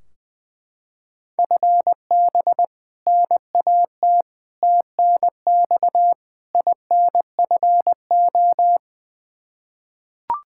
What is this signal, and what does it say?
Morse code, heard as a keyed tone, about 20 wpm, 700 Hz. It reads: FB NAT TNX INFO
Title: Practice QSO - Exchange 3 - More Info - 20wpm